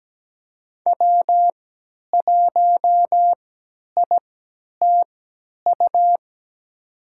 Morse code, heard as a keyed tone, about 17 wpm, spaced as for 10 wpm, 700 Hz. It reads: W1ITU
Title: USA Call Signs - 5 Characters - 3x Character Spacing - 17wpm